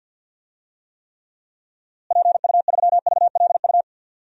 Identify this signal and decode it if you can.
Morse code, heard as a keyed tone, about 53 wpm, 700 Hz. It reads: PU3VLU
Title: DXCC-100 Call Signs - 6 Characters - Repeated 2x - 50wpm